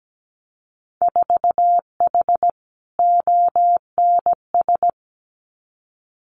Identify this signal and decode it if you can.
Morse code, heard as a keyed tone, about 17 wpm, 700 Hz. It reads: 4H ONS
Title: Winter Field Day Exchanges Encore 17wpm